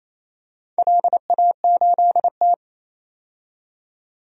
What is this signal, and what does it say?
Morse code, heard as a keyed tone, about 29 wpm, 690 Hz. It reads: LA8T
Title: DXCC-100 Call Signs - 4 Characters - 28wpm